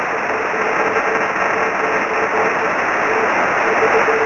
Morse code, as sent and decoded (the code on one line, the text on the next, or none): none